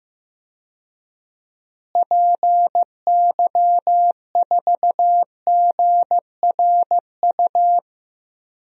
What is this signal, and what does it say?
Morse code, heard as a keyed tone, about 15 wpm, 700 Hz. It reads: PY4GRU
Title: DXCC-100 Call Signs - 6 Characters - Repeated 2x - 15wpm